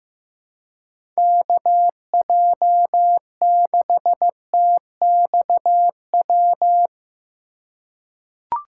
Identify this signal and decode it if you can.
Morse code, heard as a keyed tone, about 15 wpm, 700 Hz. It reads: KJ6TXW
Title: USA Call Signs - 3x Character Spacing - 15wpm